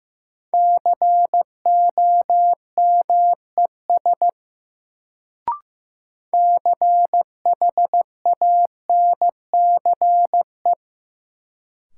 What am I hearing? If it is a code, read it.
Morse code, heard as a keyed tone, about 15 wpm, 700 Hz. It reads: COMES CHANCE
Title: Sets of 3 Words - New Word Drill - 15wpm